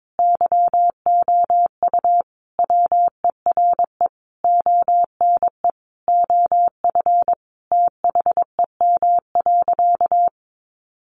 Morse code, heard as a keyed tone, about 22 wpm, 700 Hz. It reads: YOU WERE ONE OF THEM.
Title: Sentences from Top 1500 Words 22wpm